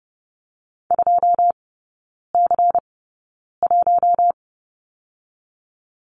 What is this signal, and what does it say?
Morse code, heard as a keyed tone, about 30 wpm, spaced as for 9 wpm, 700 Hz. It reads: 2C1
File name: Lesson 16e - Letter c - Three Characters 30wpm